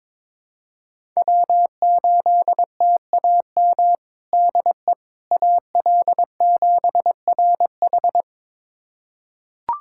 Morse code, heard as a keyed tone, about 22 wpm, 700 Hz. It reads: W8TAM DE AL7RH
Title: USA Callsign DE Callsign - 5 Characters - 22wpm